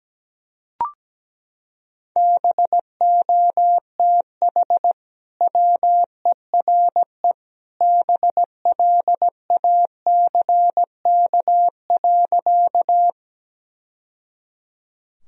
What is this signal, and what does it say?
Morse code, heard as a keyed tone, about 17 wpm, 700 Hz. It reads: BOTH WERE BLACK.
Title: Sentences from Top 1000 Words - Review Part 2 - 17wpm